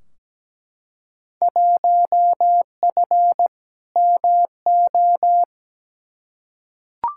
Morse code, heard as a keyed tone, about 17 wpm, 700 Hz. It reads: 1F MO
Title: Field Day Exchanges Encore - 17wpm